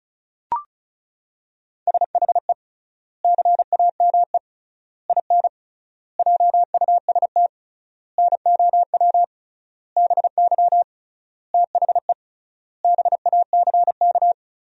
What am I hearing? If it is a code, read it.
Morse code, heard as a keyed tone, about 34 wpm, 700 Hz. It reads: SHE CAME IN JUST NOW BY THE BACK
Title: 3x Word-Spacing - Sentences from Top 300 Words - 35wpm